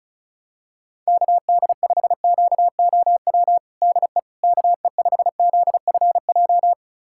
Morse code, heard as a keyed tone, about 36 wpm, 700 Hz. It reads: KD5QOW DE KE5ZFJ
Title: USA Callsign DE Callsign 35wpm